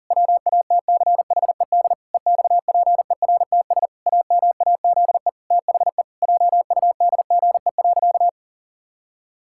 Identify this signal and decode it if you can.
Morse code, heard as a keyed tone, about 40 wpm, 700 Hz. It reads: WATCHED EXPERTS AMAZE THE JUDGE.
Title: Pangram Challenge v2 - 40wpm